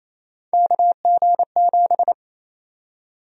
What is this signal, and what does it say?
Morse code, heard as a keyed tone, about 28 wpm, 700 Hz. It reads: KG7
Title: USA Call Sign Prefixes - 28wpm